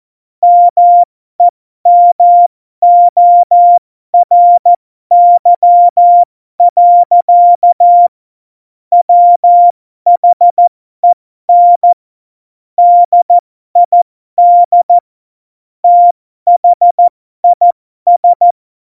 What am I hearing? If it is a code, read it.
Morse code, heard as a keyed tone, about 14 wpm, spaced as for 12 wpm, 700 Hz. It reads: MEMORY. WHEN DID THIS
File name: Morsekurs / Witze